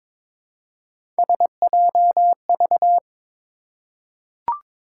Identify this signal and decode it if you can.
Morse code, heard as a keyed tone, about 21 wpm, 700 Hz. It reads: SJV
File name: ARRL Sections v2 - 22wpm